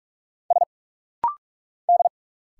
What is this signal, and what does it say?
Morse code, heard as a keyed tone, about 46 wpm, 700 Hz. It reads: S D
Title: Single Letters - Rapid-Fire 45wpm